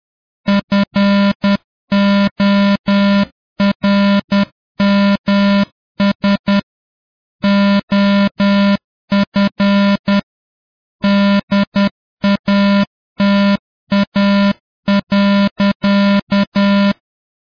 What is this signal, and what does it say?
Morse code, heard as a keyed tone, about 10 wpm, 200 Hz. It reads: FORMS OF DATA.